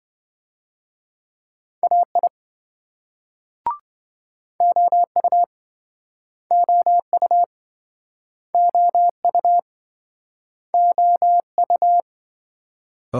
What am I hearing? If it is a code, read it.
Morse code, heard as a keyed tone, about 28 wpm, 700 Hz. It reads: AI OU OU OU OU